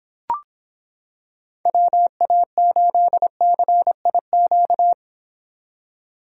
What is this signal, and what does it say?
Morse code, heard as a keyed tone, about 25 wpm, 700 Hz. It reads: WA8CIQ